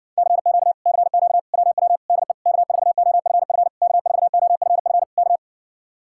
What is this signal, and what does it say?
Morse code, heard as a keyed tone, about 60 wpm, 700 Hz. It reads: CQ CQ WW DE K4OFF K4OFF K